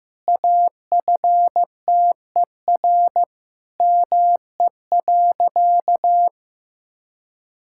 Morse code, heard as a keyed tone, about 16 wpm, 700 Hz. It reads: AFTER ME.
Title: Sentences from Top 1600 Words 15wpm